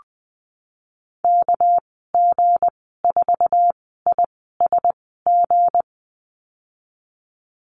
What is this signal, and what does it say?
Morse code, heard as a keyed tone, about 20 wpm, spaced as for 15 wpm, 700 Hz. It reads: KG4ISG
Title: 231 Lesson 37g - Letter x - Call Signs 20wpm